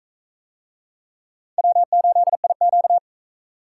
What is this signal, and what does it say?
Morse code, heard as a keyed tone, about 42 wpm, 700 Hz. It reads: W8IQ